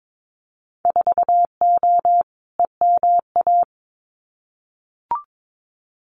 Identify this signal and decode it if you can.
Morse code, heard as a keyed tone, about 22 wpm, 700 Hz. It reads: 4O EMA
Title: Winter Field Day Exchanges Encore 22wpm